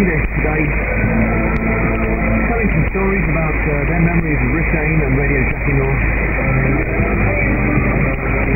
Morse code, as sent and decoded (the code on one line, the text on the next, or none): none